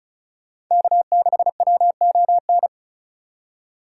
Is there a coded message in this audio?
Yes: Morse code from 0.7 to 2.7 s, heard as a keyed tone, about 46 wpm, 700 Hz. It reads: K6WON